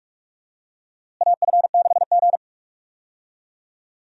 Morse code, heard as a keyed tone, about 43 wpm, 700 Hz. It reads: AF6G
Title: USA Call Signs - 4 Characters - Repeated 2x - 45wpm